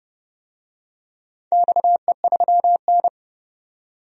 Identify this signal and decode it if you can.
Morse code, heard as a keyed tone, about 30 wpm, 700 Hz. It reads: XE3N
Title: DXCC-100 Call Signs - 4 Characters - 30wpm